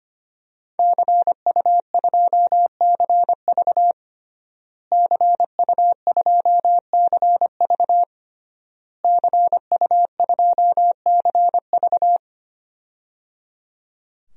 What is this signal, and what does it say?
Morse code, heard as a keyed tone, about 25 wpm, 700 Hz. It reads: CU2CV CU2CV CU2CV
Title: DXCC-100 Call Signs - 5 Characters - Repeated 3x - 25wpm